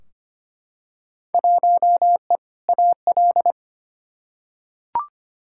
Morse code, heard as a keyed tone, about 25 wpm, 700 Hz. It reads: 1E AL
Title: Field Day Exchanges - 25wpm